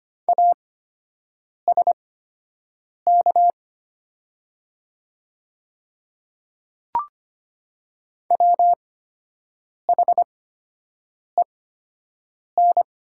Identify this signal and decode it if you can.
Morse code, heard as a keyed tone, about 25 wpm, spaced as for 7 wpm, 700 Hz. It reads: ASK WHEN
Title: 8x Character Spacing - Top 200 Words - 25wpm